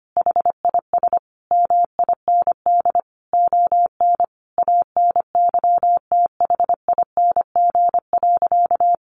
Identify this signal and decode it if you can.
Morse code, heard as a keyed tone, about 25 wpm, 700 Hz. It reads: HIS MIND ON ANYTHING.
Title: Sentences from Top 1400 Words 25wpm